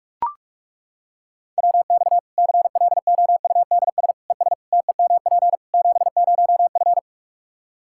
Morse code, heard as a keyed tone, about 41 wpm, 700 Hz. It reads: WX CLOUDS ES TEMP 70F
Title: Practice QSO - Exchange 2 - Info - 45wpm